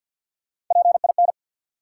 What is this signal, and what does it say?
Morse code, heard as a keyed tone, about 50 wpm, 700 Hz. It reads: PIN
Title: Three Letter Words - 50wpm